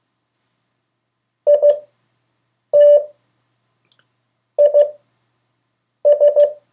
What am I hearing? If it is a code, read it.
Morse code, heard as a keyed tone, about 15 wpm, 580 Hz. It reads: I T I S
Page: CW Class Review – Delaware Valley Radio Association